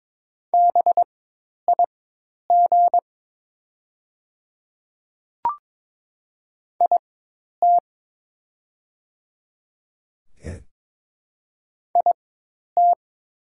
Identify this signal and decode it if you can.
Morse code, heard as a keyed tone, about 22 wpm, spaced as for 10 wpm, 690 Hz. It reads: BIG IT IT
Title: 4x Character Spacing - Top 500 Words - 22wpm